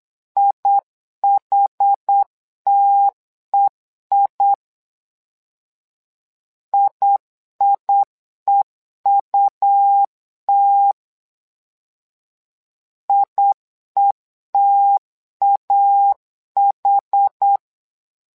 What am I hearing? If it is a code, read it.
Morse code, heard as a keyed tone, about 8 wpm, 800 Hz. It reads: IHTEI IIEUT IETAH
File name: LEZIONI / 2 UHDN / 2 UHD 8wpm 2258